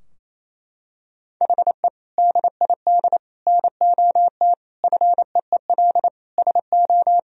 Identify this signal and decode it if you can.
Morse code, heard as a keyed tone, about 28 wpm, 700 Hz. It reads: HE DID NOT FEEL SO
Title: Sentences from Top 1000 Words - Review Part 4 - 28wpm